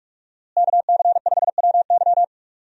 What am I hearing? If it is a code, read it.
Morse code, heard as a keyed tone, about 45 wpm, 700 Hz. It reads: KC5WY
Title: USA Call Signs - 5 Characters - Repeated 3x - 45wpm